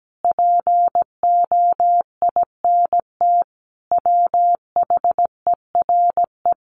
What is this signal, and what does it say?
Morse code, heard as a keyed tone, about 17 wpm, 700 Hz. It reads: POINT WHERE
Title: Sets of 4 Words 17wpm